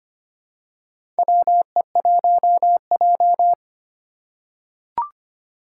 Morse code, heard as a keyed tone, about 25 wpm, 700 Hz. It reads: WE1J
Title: USA Call Signs - 4 Characters - 4x Character Spacing - 25wpm